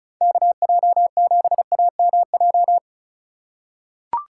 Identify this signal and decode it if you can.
Morse code, heard as a keyed tone, about 36 wpm, 700 Hz. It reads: KJ7AMJ
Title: USA Call Signs - 3x Character Spacing - 35wpm